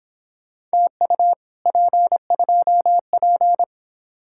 Morse code, heard as a keyed tone, about 26 wpm, 700 Hz. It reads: TU P2P
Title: Speed-Racing Common QSO Phrases Encore - 22wpm